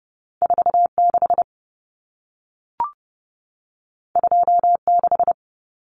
Morse code, heard as a keyed tone, about 30 wpm, 700 Hz. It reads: 46 26
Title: Common Cut Numbers - 2-Digit 30wpm